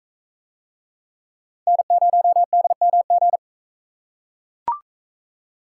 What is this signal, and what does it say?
Morse code, heard as a keyed tone, about 42 wpm, 700 Hz. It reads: N0DMG